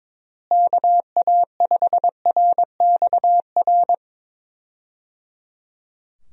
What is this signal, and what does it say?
Morse code, heard as a keyed tone, about 22 wpm, 700 Hz. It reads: KA5RXR
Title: USA Call Signs - 6 Characters - Repeated 3x - 22wpm